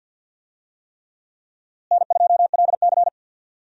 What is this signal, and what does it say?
Morse code, heard as a keyed tone, about 50 wpm, 700 Hz. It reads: N2LC
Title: USA Call Signs - 4 Characters - Repeated 3x - 50wpm